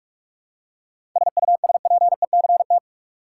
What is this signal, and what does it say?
Morse code, heard as a keyed tone, about 46 wpm, 700 Hz. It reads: SUSPECT
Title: Seven Letter Words - 45wpm